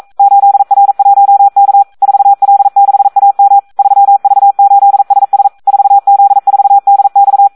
Morse code, heard as a keyed tone, about 42 wpm, 800 Hz. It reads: 8R1K 4L6AM 3V8SS 4Z4DX
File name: Audio practica CW / SD 1 42